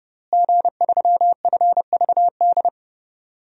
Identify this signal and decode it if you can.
Morse code, heard as a keyed tone, about 31 wpm, 700 Hz. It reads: G3FVD
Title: DXCC-100 Call Signs - 5 Characters - Repeated 3x - 30wpm